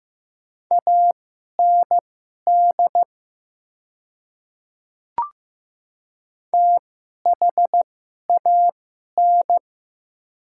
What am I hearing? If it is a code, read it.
Morse code, heard as a keyed tone, about 15 wpm, spaced as for 11 wpm, 700 Hz. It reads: AND THAN
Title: 2x Character Spacing - Top 100 Words - 15wpm